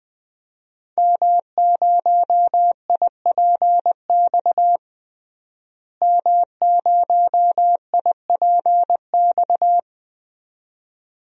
Morse code, heard as a keyed tone, about 20 wpm, 700 Hz. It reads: M0IPX M0IPX
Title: DXCC-100 Call Signs - 5 Characters - Repeated 2x - 20wpm